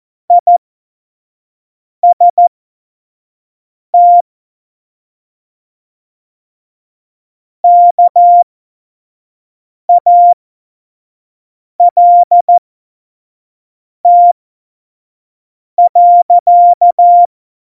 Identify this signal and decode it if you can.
Morse code, heard as a keyed tone, about 14 wpm, spaced as for 5 wpm, 700 Hz. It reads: IST KALT.
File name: Morsekurs / Witze / Witz 24. September 2019, 25 F70